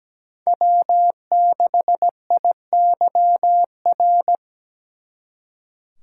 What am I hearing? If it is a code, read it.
Morse code, heard as a keyed tone, about 20 wpm, 700 Hz. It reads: W6IYR
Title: Speed Racing - USA 1x3 Call Signs 17wpm